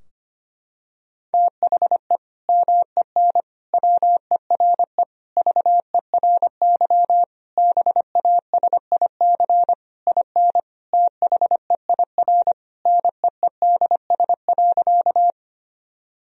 Morse code, heard as a keyed tone, about 25 wpm, 700 Hz. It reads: THE MEN WERE VERY BASIC IN THEIR NEEDS.
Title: Sentences from Top 1000 Words - Review Part 3 - 25wpm